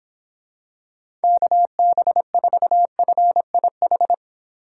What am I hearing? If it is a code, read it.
Morse code, heard as a keyed tone, about 26 wpm, 700 Hz. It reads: KB4FIH
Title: Speed Racing - USA 2x3 Call Signs 22wpm